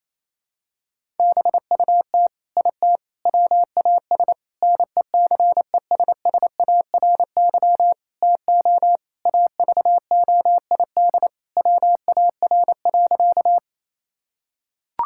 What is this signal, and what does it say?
Morse code, heard as a keyed tone, about 28 wpm, 700 Hz. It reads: BUT IT WAS NECESSARY TO AVOID WAR.
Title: Sentences from Top 1000 Words - Review Part 6 - 28wpm